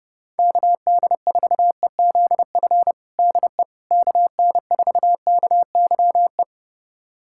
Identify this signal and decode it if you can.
Morse code, heard as a keyed tone, about 32 wpm, 690 Hz. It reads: KD4EZF DE KN4KYE
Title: USA Callsign DE Callsign 30wpm